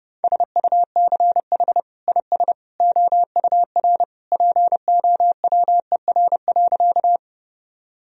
Morse code, heard as a keyed tone, about 32 wpm, 700 Hz. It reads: SUCH IS OUR POWER.
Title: Sentences from Top 1000 Words - Review Part 2 - 30wpm